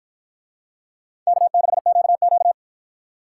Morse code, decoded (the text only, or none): K6QQ